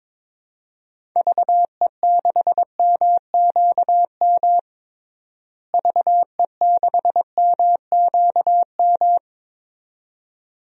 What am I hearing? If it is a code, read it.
Morse code, heard as a keyed tone, about 22 wpm, 700 Hz. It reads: VE6MQM VE6MQM
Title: DXCC-100 Call Signs - 6 Characters - Repeated 2x - 22wpm